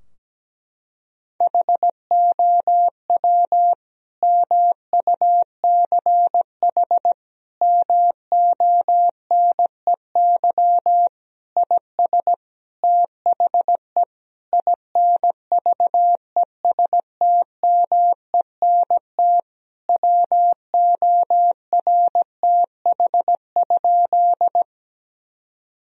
Morse code, heard as a keyed tone, about 17 wpm, 700 Hz. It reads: HOW MUCH MONEY IS THE INVESTMENT WORTH?
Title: Sentences from Top 1000 Words - Review Part 4 - 17wpm